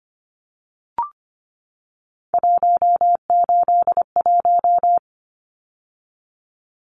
Morse code, heard as a keyed tone, about 25 wpm, 700 Hz. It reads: 181
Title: Numbers - 3-Digit 25wpm